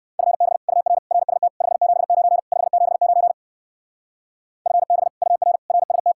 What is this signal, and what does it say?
Morse code, heard as a keyed tone, about 58 wpm, 700 Hz. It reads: FB UR RST 579 579 FB UR RST